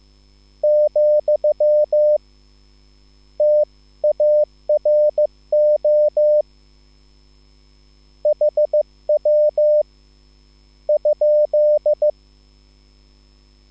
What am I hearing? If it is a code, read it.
Morse code, heard as a keyed tone, about 15 wpm, spaced as for 11 wpm, 600 Hz. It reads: , TARO HW?
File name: QSL / CW lesson